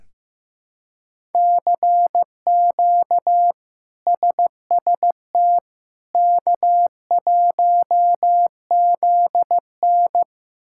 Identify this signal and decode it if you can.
Morse code, heard as a keyed tone, about 16 wpm, 700 Hz. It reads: CQ SST K1ZN
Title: K1USN SST QSO Practice - 15wpm